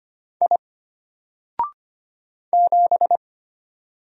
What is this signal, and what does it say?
Morse code, heard as a keyed tone, about 25 wpm, 700 Hz. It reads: I 7
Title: Single Character (Letter:Number:Punctuation:Prosign) Alt 25wpm